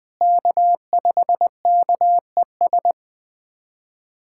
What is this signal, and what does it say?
Morse code, heard as a keyed tone, about 20 wpm, 700 Hz. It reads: K5KES
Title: USA Callsign DE Callsign - 5 Characters - 20wpm